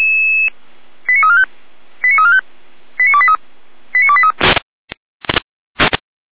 Five-tone selective call (ZVEI), heard as quick runs of five tones, tones of about 70 ms. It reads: 89345 89345 89283 89283